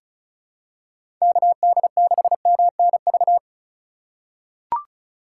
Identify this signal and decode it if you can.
Morse code, heard as a keyed tone, about 34 wpm, 700 Hz. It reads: KD6MNV
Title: USA Call Signs - 6 Characters - 2x Character Spacing - 35wpm